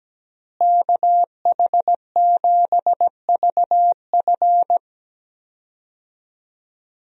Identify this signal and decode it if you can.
Morse code, heard as a keyed tone, about 17 wpm, 700 Hz. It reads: KH7VF
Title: USA Call Signs - 5 Characters - Repeated 2x - 17wpm